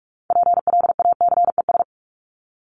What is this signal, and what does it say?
Morse code, heard as a keyed tone, about 44 wpm, 700 Hz. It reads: PLACES
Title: ES:S Suffix - Top 100 Words with Base Words 45wpm